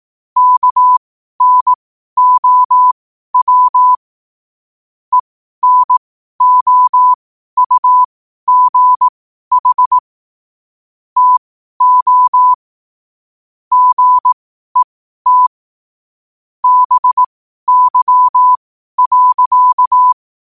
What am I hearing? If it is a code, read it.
Morse code, heard as a keyed tone, about 18 wpm, spaced as for 12 wpm, 1000 Hz. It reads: KNOW ENOUGH TO GET BY.
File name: qotdfiles / 12wpm